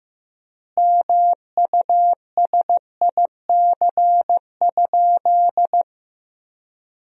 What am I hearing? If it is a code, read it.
Morse code, heard as a keyed tone, about 16 wpm, 700 Hz. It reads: MUSIC?
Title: 4x Word-Spacing - Sentences from Top 300 Words - 15wpm